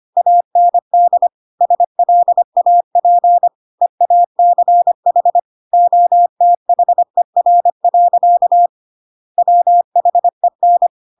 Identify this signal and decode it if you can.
Morse code, heard as a keyed tone, about 25 wpm, 700 Hz. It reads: AND SLAP EACH OTHER. WHEN
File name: Pinocchio / Pinocchio ch2 25 25 ik1hge